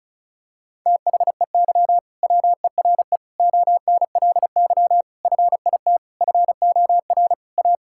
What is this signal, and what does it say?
Morse code, heard as a keyed tone, about 34 wpm, 700 Hz. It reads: THEY WERE ONLY FIT FOR A
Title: Sentences from Top 1000 Words - Review Part 5 - 35wpm